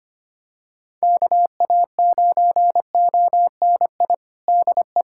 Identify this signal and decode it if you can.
Morse code, heard as a keyed tone, about 25 wpm, 700 Hz. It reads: KA9ONI DE
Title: USA Callsign DE Callsign - 6 Characters - 25wpm